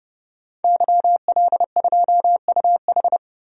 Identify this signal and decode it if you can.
Morse code, heard as a keyed tone, about 32 wpm, 700 Hz. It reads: YL2UH
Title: DXCC-100 Call Signs - 5 Characters - 30wpm